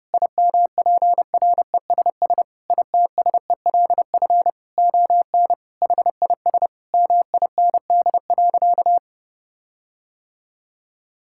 Morse code, heard as a keyed tone, about 32 wpm, 700 Hz. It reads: IMPRESS ITSELF ON HIS MIND.
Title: Sentences from Top 1600 Words 30wpm